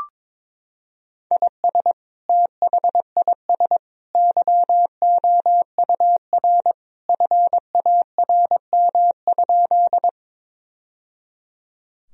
Morse code, heard as a keyed tone, about 22 wpm, 700 Hz. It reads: IS THIS YOUR FARM?
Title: Sentences from Top 1000 Words - Review Part 2 - 22wpm